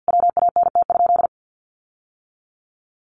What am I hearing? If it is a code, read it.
Morse code, heard as a keyed tone, about 50 wpm, 700 Hz. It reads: WANT?
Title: Sentences from Top 800 Words 50wpm